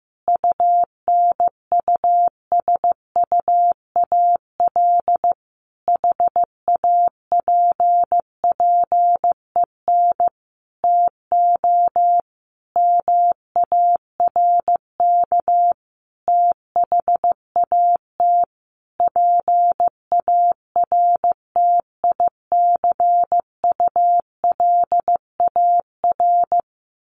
Morse code, heard as a keyed tone, about 14 wpm, 700 Hz. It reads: UNUSUAL HAPPEN TO MARK THAT PARTICULAR
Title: Sentences from Top 1400 Words 15wpm